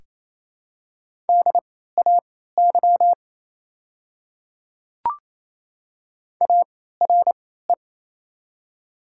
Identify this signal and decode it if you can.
Morse code, heard as a keyed tone, about 29 wpm, spaced as for 16 wpm, 690 Hz. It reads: DAY ARE
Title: 3x Character Spacing - Top 100 Words - 28wpm